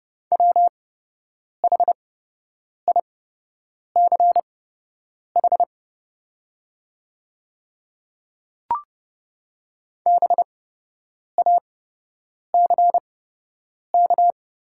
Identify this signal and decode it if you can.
Morse code, heard as a keyed tone, about 30 wpm, spaced as for 8 wpm, 700 Hz. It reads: WHICH BACK
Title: 8x Character Spacing - Top 300 Words - 30wpm